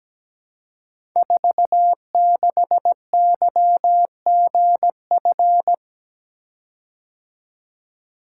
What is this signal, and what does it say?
Morse code, heard as a keyed tone, about 17 wpm, 700 Hz. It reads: 46YGF
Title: Code Groups - 5 Letter:Numbers - 17wpm